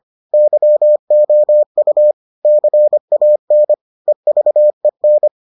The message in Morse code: YOU CAN EVEN

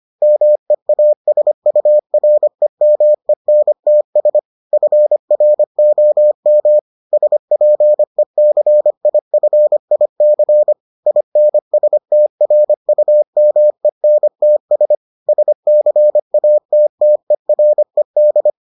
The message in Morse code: MEASUREMENTS FROM SPECIFIC INSTRUMENTS SCATTERED

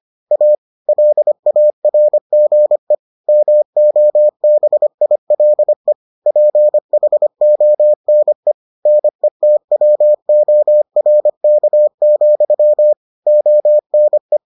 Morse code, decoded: A LARGE MOBILE PHONE NETWORK, ONE

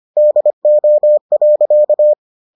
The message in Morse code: DO.

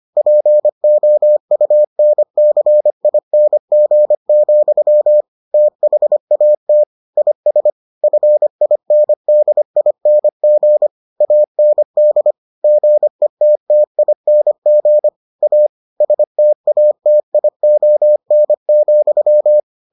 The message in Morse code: POUNCING, THAT IS FINDING AND GETTING A STATION,

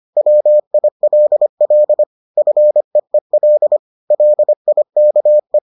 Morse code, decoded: WILL FEEL LIKE

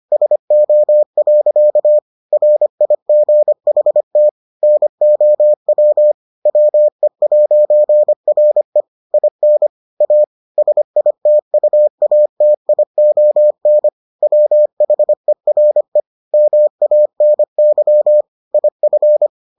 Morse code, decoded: SO. RIGHT NOW WE'RE IN A SITUATION WHERE MANY IF

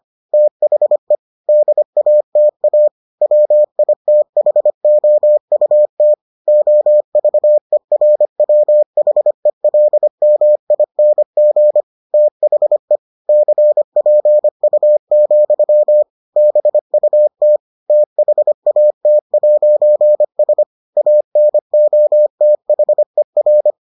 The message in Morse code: THE DATA WITHOUT OVERWHELMING THE CPU, BUT THAT'S ANOTHER

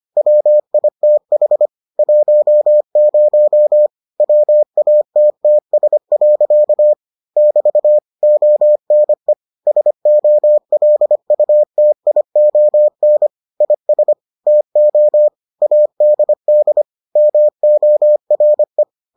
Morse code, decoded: WITH 10 WATTS. <BT> ONE SOLUTION IS TO ADD MORE

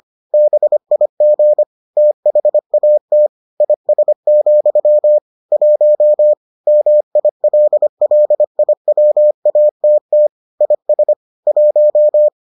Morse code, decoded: BIG THAT IS, 1 MILLIWATT IS 1

